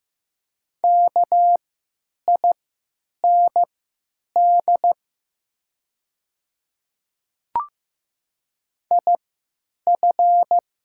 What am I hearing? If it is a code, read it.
Morse code, heard as a keyed tone, about 15 wpm, spaced as for 9 wpm, 700 Hz. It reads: KIND IF
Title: 3x Character Spacing - Top 300 Words - 15wpm